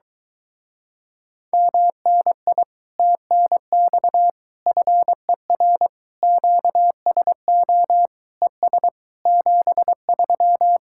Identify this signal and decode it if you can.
Morse code, heard as a keyed tone, about 22 wpm, 700 Hz. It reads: MNI TNX FER QSO ES 73